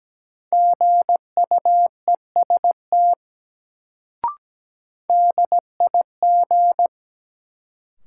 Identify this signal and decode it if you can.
Morse code, heard as a keyed tone, about 17 wpm, 700 Hz. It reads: GUEST DIG
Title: Sentences from Top 1600 Words - New Word Drill Alt - 17wpm